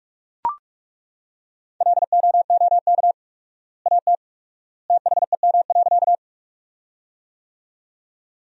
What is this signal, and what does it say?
Morse code, heard as a keyed tone, about 41 wpm, 700 Hz. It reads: LOOK AT THEM.
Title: 4x Word-Spacing - Sentences from Top 100 Words - 45wpm